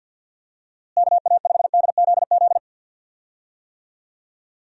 Morse code, heard as a keyed tone, about 50 wpm, 700 Hz. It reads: KA5DZZ
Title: USA Call Signs - 6 Characters - Repeated 2x - 50wpm